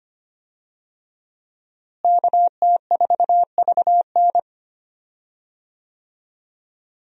Morse code, heard as a keyed tone, about 25 wpm, 700 Hz. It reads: KT4VN
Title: USA Call Signs - 5 Characters - Repeated 2x - 25wpm